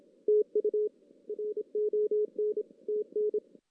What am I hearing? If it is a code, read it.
Morse code, heard as a keyed tone, about 26 wpm, 420 Hz. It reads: TU RON TN
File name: 2016 JAN CW NAQP / 2016 Jan naqp k5ot - 0-5